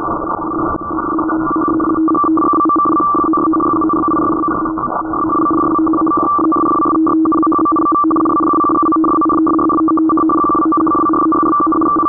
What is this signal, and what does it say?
Two-tone teleprinter signal, about 850 Hz shift.